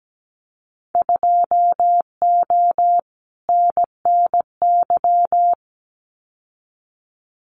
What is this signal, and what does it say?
Morse code, heard as a keyed tone, about 17 wpm, 700 Hz. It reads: 2O NNY
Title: Winter Field Day Exchanges 17wpm